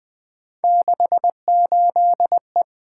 Morse code, heard as a keyed tone, about 20 wpm, 700 Hz. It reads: 68E
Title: Three Letter-Number Combinations - 20wpm